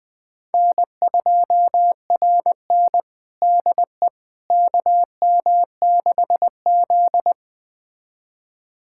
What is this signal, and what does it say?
Morse code, heard as a keyed tone, about 20 wpm, 700 Hz. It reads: N2RN DE KM6Z
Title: USA Callsign DE Callsign - 4 Characters - 20wpm